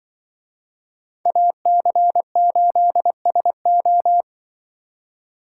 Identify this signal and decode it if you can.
Morse code, heard as a keyed tone, about 24 wpm, 700 Hz. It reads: AC8SO
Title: Speed Racing - USA 2x2 Call Signs 20wpm